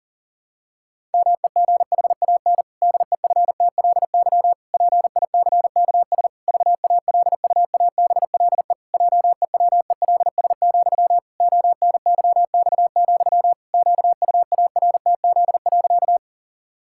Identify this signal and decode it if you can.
Morse code, heard as a keyed tone, about 40 wpm, 700 Hz. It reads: MEGHAN DEFTLY PICKS VALUABLE JEWELS, ONYX, QUARTZ.
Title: Pangram Challenge v2 - 40wpm